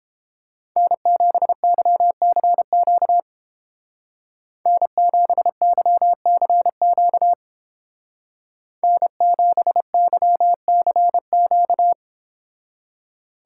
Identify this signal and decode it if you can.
Morse code, heard as a keyed tone, about 29 wpm, 700 Hz. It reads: N7YCQ N7YCQ N7YCQ